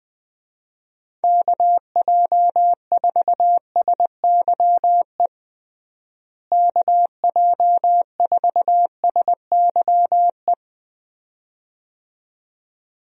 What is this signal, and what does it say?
Morse code, heard as a keyed tone, about 20 wpm, 700 Hz. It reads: KJ4SYE KJ4SYE
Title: USA Call Signs - 6 Characters - Repeated 2x - 20wpm